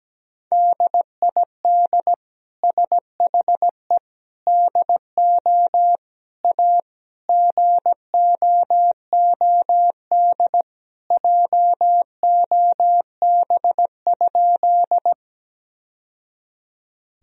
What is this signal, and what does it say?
Morse code, heard as a keyed tone, about 17 wpm, 700 Hz. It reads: DID SHE DO A GOOD JOB?
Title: Sentences from Top 1000 Words - Review Part 3 - 17wpm